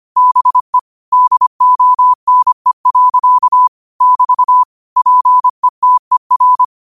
Morse code, heard as a keyed tone, about 25 wpm, 1000 Hz. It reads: BE DONE. <BT> PETER